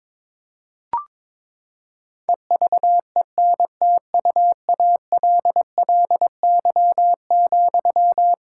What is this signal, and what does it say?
Morse code, heard as a keyed tone, about 22 wpm, 700 Hz. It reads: EVENTUALLY,
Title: Sentences from Top 1000 Words - Review Part 6 - 22wpm